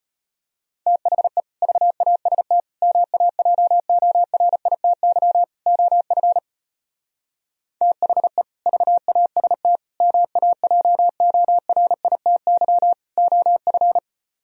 Morse code, heard as a keyed tone, about 33 wpm, 700 Hz. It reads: THE VAST MAJORITY OF THE VAST MAJORITY OF